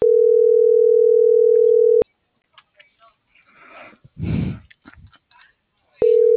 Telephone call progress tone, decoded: ringback tone

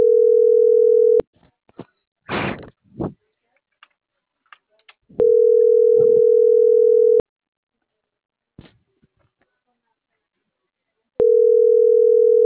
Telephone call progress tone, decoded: ringback tone

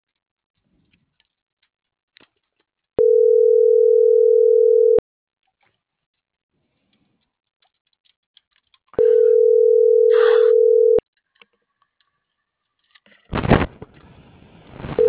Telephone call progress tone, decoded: ringback tone

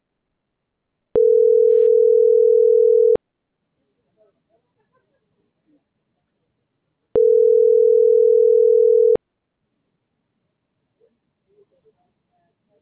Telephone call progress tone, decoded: ringback tone

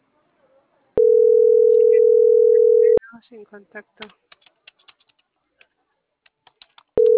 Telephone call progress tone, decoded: ringback tone

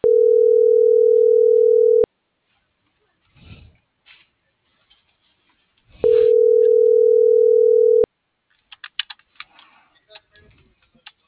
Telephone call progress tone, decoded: ringback tone

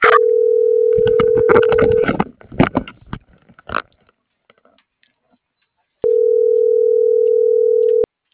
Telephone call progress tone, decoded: ringback tone

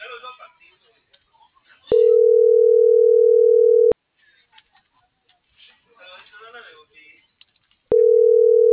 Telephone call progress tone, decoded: ringback tone

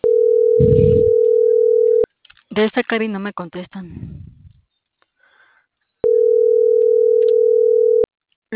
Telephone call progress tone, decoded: ringback tone

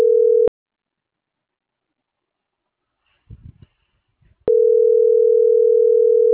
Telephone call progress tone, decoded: ringback tone

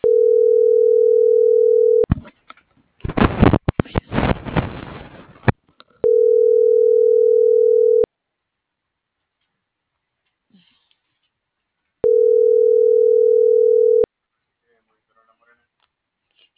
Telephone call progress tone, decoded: ringback tone

